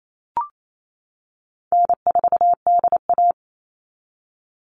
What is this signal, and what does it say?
Morse code, heard as a keyed tone, about 28 wpm, 700 Hz. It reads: N4DA
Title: NA SOTA Chaser Callsigns 28wpm